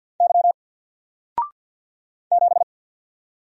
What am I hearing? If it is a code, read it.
Morse code, heard as a keyed tone, about 48 wpm, 700 Hz. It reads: Y 7
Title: Single Character (Letter:Number:Punctuation:Prosign) Alt 50wpm